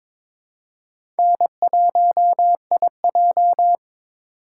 Morse code, heard as a keyed tone, about 22 wpm, 700 Hz. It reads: N1IJ
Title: USA Call Signs - 4 Characters - 4x Character Spacing - 22wpm